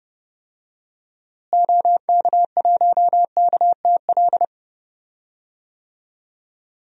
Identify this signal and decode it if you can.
Morse code, heard as a keyed tone, about 30 wpm, 700 Hz. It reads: OK1KTL
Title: DXCC-100 Call Signs - 6 Characters - Repeated 2x - 30wpm